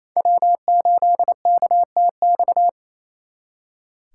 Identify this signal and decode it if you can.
Morse code, heard as a keyed tone, about 34 wpm, 700 Hz. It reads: W8KTX
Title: Speed Racing - USA 1x3 Call Signs 28wpm